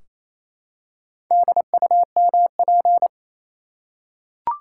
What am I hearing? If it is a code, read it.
Morse code, heard as a keyed tone, about 28 wpm, 700 Hz. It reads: DUMP
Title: Four Letter Words - 28wpm